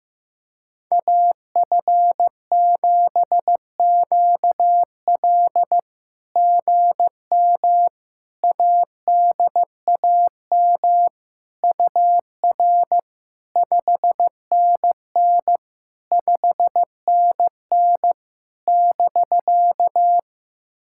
Morse code, heard as a keyed tone, about 17 wpm, 700 Hz. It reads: AF7QL GM ADAM UR 5NN 5NN <BK>
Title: SOTA QSO Practice 15wpm